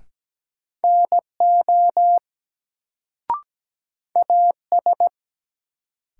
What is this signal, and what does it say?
Morse code, heard as a keyed tone, about 17 wpm, 700 Hz. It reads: NO AS
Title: Sets of 2 Words - New Word Drill - 17wpm